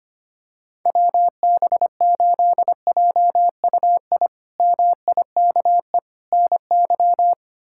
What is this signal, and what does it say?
Morse code, heard as a keyed tone, about 25 wpm, 700 Hz. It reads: WB8JUI MIKE NY
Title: K1USN SST QSO Practice - 25wpm